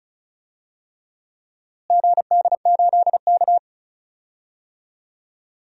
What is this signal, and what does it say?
Morse code, heard as a keyed tone, about 34 wpm, 700 Hz. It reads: GD8K
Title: DXCC-100 Call Signs - 4 Characters - Repeated 3x - 35wpm